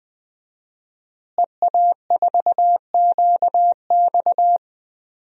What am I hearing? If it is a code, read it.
Morse code, heard as a keyed tone, about 20 wpm, 700 Hz. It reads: EA4QX
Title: DXCC-100 Call Signs - 5 Characters - Repeated 3x - 20wpm